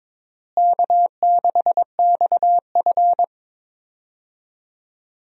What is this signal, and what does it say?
Morse code, heard as a keyed tone, about 22 wpm, 700 Hz. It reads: K6XF
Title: USA Call Signs - 4 Characters - Repeated 2x - 22wpm